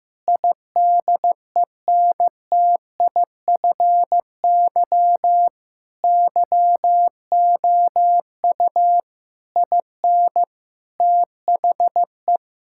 Morse code, heard as a keyed tone, about 15 wpm, 700 Hz. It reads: IDENTIFY YOU IN THE
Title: Sentences from Top 1000 Words - Review Part 6 - 15wpm